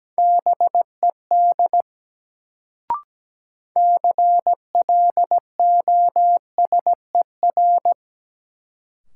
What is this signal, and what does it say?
Morse code, heard as a keyed tone, about 17 wpm, 700 Hz. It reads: BED CLOSER
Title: Sets of 5 Words Encore - New Word Drill - 17wpm